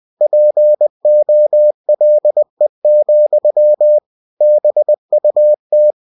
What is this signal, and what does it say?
Morse code, heard as a keyed tone, about 20 wpm, 600 Hz. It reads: POLE, BUT
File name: cw-books / winnie-the-pooh / winnie-20wpm-page-0093